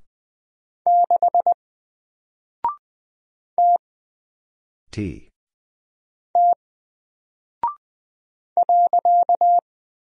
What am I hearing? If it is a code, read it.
Morse code, heard as a keyed tone, about 20 wpm, 700 Hz. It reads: 6 T T .
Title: Single Character (Letter:Number:Punctuation:Prosign) Alt 20wpm